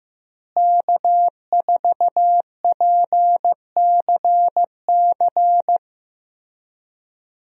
Morse code, heard as a keyed tone, about 15 wpm, 700 Hz. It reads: K4PCC
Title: USA Call Signs - 5 Characters - Repeated 3x - 15wpm